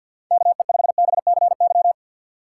Morse code, heard as a keyed tone, about 52 wpm, 700 Hz. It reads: KE5BCY
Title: K1USN SST QSO Practice - 50wpm